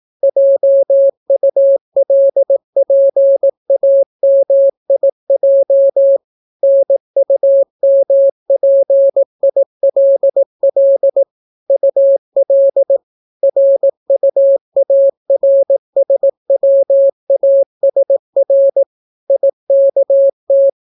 Morse code, heard as a keyed tone, about 18 wpm, 550 Hz. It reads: JULPAMIJ NUMPILL UL RUARSWASR IKT